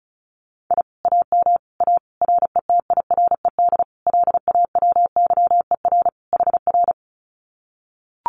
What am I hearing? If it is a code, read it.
Morse code, heard as a keyed tone, about 36 wpm, 700 Hz. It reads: I AM A RETIRED LAWYER HR
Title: Common QSO Phrases Encore 35wpm